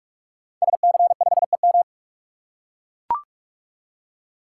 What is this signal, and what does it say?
Morse code, heard as a keyed tone, about 44 wpm, 700 Hz. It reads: SC5EM
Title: Code Groups - 5 Letter:Numbers - 45wpm